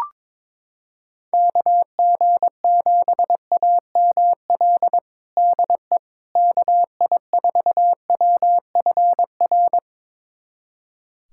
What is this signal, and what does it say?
Morse code, heard as a keyed tone, about 22 wpm, 700 Hz. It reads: KG7AML DE KI4WFR
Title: USA Callsign DE Callsign 22wpm